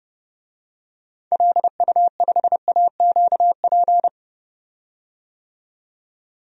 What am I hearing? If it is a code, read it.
Morse code, heard as a keyed tone, about 31 wpm, 700 Hz. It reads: LU5AQP
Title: DXCC-100 Call Signs - 6 Characters - 30wpm